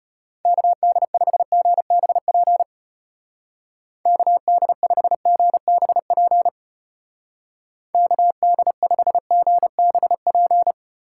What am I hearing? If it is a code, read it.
Morse code, heard as a keyed tone, about 31 wpm, 700 Hz. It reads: KD5GBP KD5GBP KD5GBP
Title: Speed Racing - USA 2x3 Call Signs 25wpm